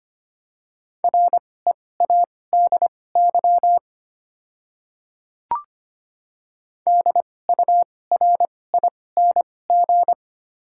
Morse code, heard as a keyed tone, about 25 wpm, spaced as for 18 wpm, 700 Hz. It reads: READY DURING
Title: 2x Character Spacing - Top 500 Words - 25wpm